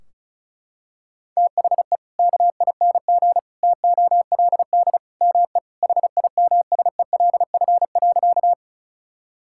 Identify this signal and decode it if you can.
Morse code, heard as a keyed tone, about 34 wpm, 700 Hz. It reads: THE KING TOLD ME HIMSELF.
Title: Sentences from Top 1600 Words 35wpm